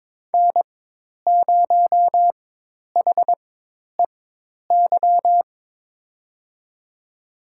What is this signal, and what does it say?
Morse code, heard as a keyed tone, about 22 wpm, spaced as for 10 wpm, 700 Hz. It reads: N0HEY
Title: USA Call Signs - 5 Characters - 4x Character Spacing - 22wpm